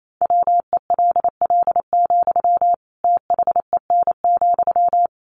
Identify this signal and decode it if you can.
Morse code, heard as a keyed tone, about 28 wpm, 700 Hz. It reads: WELL, THEN,
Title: Sentences from Top 700 Words 28wpm